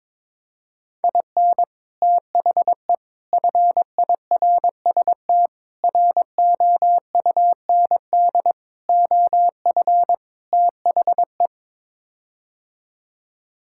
Morse code, heard as a keyed tone, about 22 wpm, 700 Hz. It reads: IN THE FIRST ROUND OF THE
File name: practice / 22wpm / Sets of 6 Words - 22wpm